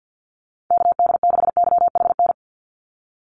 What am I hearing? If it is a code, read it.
Morse code, heard as a keyed tone, about 50 wpm, 700 Hz. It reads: KD6YHN
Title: USA Call Signs 50wpm